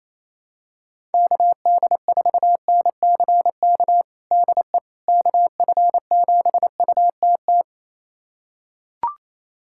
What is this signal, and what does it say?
Morse code, heard as a keyed tone, about 28 wpm, 700 Hz. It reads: KD4NCK DE KF7UTT
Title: USA Callsign DE Callsign 28wpm